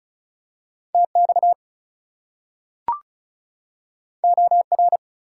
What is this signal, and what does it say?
Morse code, heard as a keyed tone, about 36 wpm, 700 Hz. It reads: TX OR
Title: US States and Canadian Provinces 35wpm